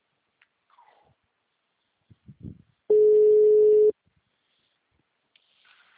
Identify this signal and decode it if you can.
Ringback tone (Europe, 425 Hz).